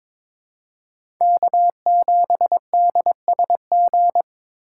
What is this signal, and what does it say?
Morse code, heard as a keyed tone, about 22 wpm, 700 Hz. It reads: K7DSG
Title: USA Call Signs - 5 Characters - 3x Character Spacing - 22wpm